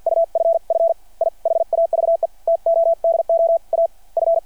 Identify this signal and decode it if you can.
Morse code, heard as a keyed tone, about 52 wpm, spaced as for 45 wpm, 660 Hz. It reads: UFF I HAVE TO DO A F
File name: archived / CRSnet